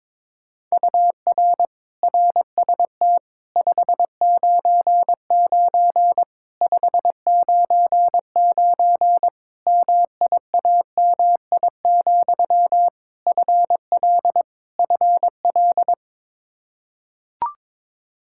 Morse code, heard as a keyed tone, about 22 wpm, 700 Hz. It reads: UR RST 599 599 MIAMI, FL FL
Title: Practice QSO - Exchange 1 - Beginning - 22wpm